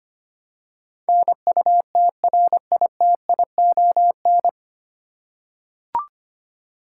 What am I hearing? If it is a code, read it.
Morse code, heard as a keyed tone, about 25 wpm, 700 Hz. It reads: NUTRITION
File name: practice / 25wpm / 9 Letter Words - First 3 Letters Given - 25wpm